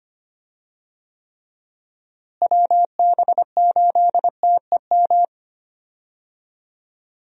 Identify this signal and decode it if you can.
Morse code, heard as a keyed tone, about 25 wpm, 700 Hz. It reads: WB8TEM
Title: USA Call Signs - 6 Characters - Repeated 3x - 25wpm